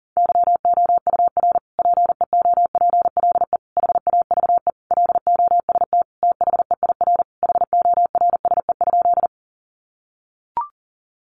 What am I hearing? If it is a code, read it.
Morse code, heard as a keyed tone, about 40 wpm, 700 Hz. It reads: YOUR PEOPLE HAVE LOST THEIR HORSE?
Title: Sentences from Top 1400 Words 40wpm